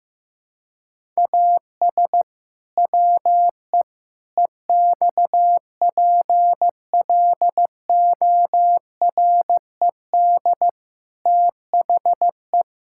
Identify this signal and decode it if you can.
Morse code, heard as a keyed tone, about 15 wpm, 700 Hz. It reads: AS WE EXPLORED THE
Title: Pangram Challenge v2 - 15wpm